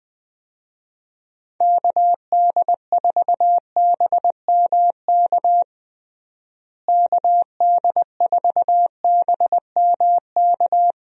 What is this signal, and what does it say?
Morse code, heard as a keyed tone, about 21 wpm, 700 Hz. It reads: KD4BMK KD4BMK
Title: USA Call Signs - 6 Characters - Repeated 3x - 20wpm